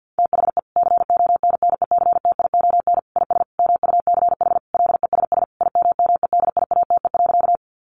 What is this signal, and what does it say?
Morse code, heard as a keyed tone, about 50 wpm, 700 Hz. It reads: THE CONNECTION IS MUCH LESS IMMEDIATE.
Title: Sentences from Top 1300 Words 50wpm